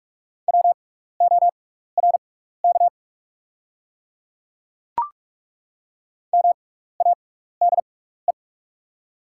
Morse code, heard as a keyed tone, about 42 wpm, spaced as for 15 wpm, 700 Hz. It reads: WORK MADE
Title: 6x Character Spacing - Top 300 Words - 45wpm